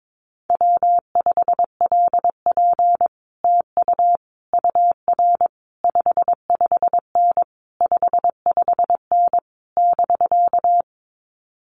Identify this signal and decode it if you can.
Morse code, heard as a keyed tone, about 22 wpm, 700 Hz. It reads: W5LP TU UR 55N 55N <BK>
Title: Common QSO Phrases Encore 22wpm